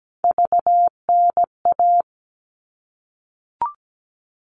Morse code, heard as a keyed tone, about 17 wpm, 700 Hz. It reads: VNA
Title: All Cut Numbers - 3-Digit 17wpm